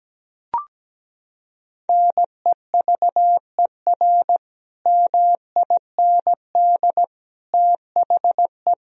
Morse code, heard as a keyed tone, about 17 wpm, 700 Hz. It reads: NEVER MIND THE
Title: Sentences from Top 1000 Words - Review Part 1 - 17wpm